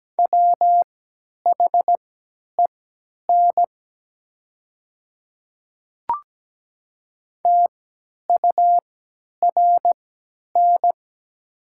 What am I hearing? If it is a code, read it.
Morse code, heard as a keyed tone, about 17 wpm, spaced as for 10 wpm, 700 Hz. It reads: WHEN TURN
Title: 3x Character Spacing - Top 200 Words - 17wpm